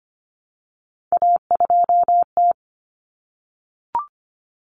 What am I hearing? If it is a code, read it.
Morse code, heard as a keyed tone, about 25 wpm, 700 Hz. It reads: A2T
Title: Common Cut Numbers - 3-Digit 25wpm